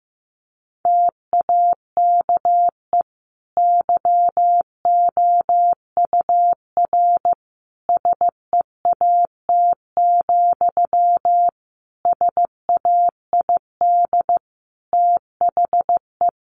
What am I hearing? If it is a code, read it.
Morse code, heard as a keyed tone, about 15 wpm, 700 Hz. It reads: TAKE YOUR SEAT, SAID THE
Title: Sentences from Top 1500 Words 15wpm